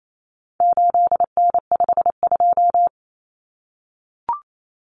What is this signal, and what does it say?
Morse code, heard as a keyed tone, about 28 wpm, 700 Hz. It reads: 8N52
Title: Common Cut Numbers - 4-Digit 28wpm